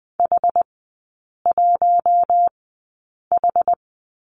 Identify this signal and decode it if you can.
Morse code, heard as a keyed tone, about 20 wpm, spaced as for 8 wpm, 700 Hz. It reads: H1H
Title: 060 Lesson 11e - Letter h - Three Characters 20wpm